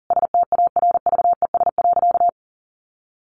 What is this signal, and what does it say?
Morse code, heard as a keyed tone, about 40 wpm, 700 Hz. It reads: STARVES.
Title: A Sound Of Thunder 40wpm